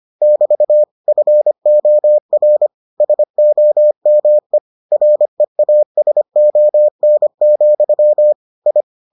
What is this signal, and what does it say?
Morse code, heard as a keyed tone, about 25 wpm, 600 Hz. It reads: <BT> FOR SOME REASON, I